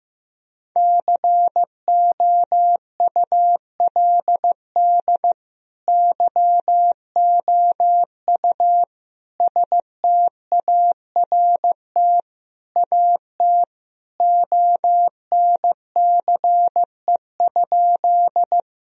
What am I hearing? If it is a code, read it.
Morse code, heard as a keyed tone, about 16 wpm, 700 Hz. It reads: COULD YOU START AT ONCE?
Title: Sentences from Top 1000 Words - Review Part 1 - 15wpm